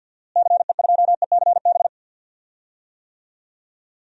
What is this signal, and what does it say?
Morse code, heard as a keyed tone, about 51 wpm, 700 Hz. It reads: CE2ECB
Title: DXCC-100 Call Signs - 6 Characters - Repeated 3x - 50wpm